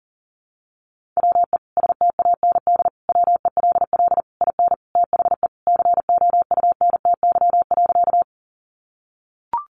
Morse code, heard as a keyed tone, about 40 wpm, 700 Hz. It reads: WE STAND WELL IN THE COUNTY.
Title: Sentences from Top 1200 Words 40wpm